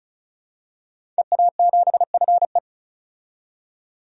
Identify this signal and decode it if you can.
Morse code, heard as a keyed tone, about 36 wpm, 700 Hz. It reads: EA7FE
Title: DXCC-100 Call Signs - 5 Characters - 35wpm